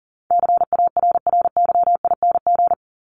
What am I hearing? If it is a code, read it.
Morse code, heard as a keyed tone, about 38 wpm, 700 Hz. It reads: CARRYING
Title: ING Suffix - Top 100 Words with Base Words 40wpm